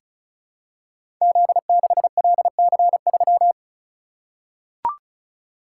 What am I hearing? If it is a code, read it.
Morse code, heard as a keyed tone, about 34 wpm, 700 Hz. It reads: Z6LC3
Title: Code Groups - 5 Letter:Numbers - 35wpm